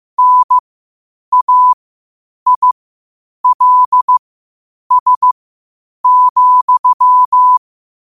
Morse code, heard as a keyed tone, about 15 wpm, spaced as for 8 wpm, 1000 Hz. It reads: NAILS,
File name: qotdfiles / 8wpm